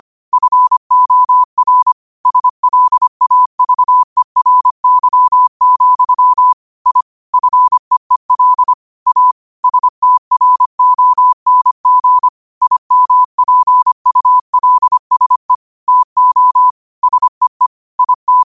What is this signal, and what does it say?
Morse code, heard as a keyed tone, about 25 wpm, 1000 Hz. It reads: FOR SLAVERY, I FEEL A STRONG IMPULSE TO SEE IT
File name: qotdfiles / 25wpm